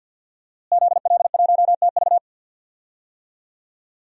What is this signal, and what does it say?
Morse code, heard as a keyed tone, about 47 wpm, 700 Hz. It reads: ZL1TV
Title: DXCC-100 Call Signs - 5 Characters - Repeated 3x - 50wpm